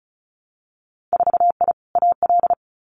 Morse code, heard as a keyed tone, about 36 wpm, 700 Hz. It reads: 4I AL
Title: Winter Field Day Exchanges Encore 35wpm